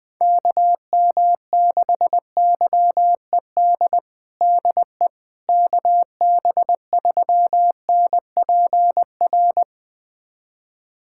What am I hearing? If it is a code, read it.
Morse code, heard as a keyed tone, about 21 wpm, 700 Hz. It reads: KM6YED DE KB3NPR
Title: USA Callsign DE Callsign - 6 Characters - 20wpm